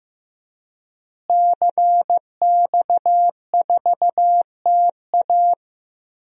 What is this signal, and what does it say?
Morse code, heard as a keyed tone, about 15 wpm, 700 Hz. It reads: CX4TA
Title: DXCC-100 Call Signs - 5 Characters - Repeated 3x - 15wpm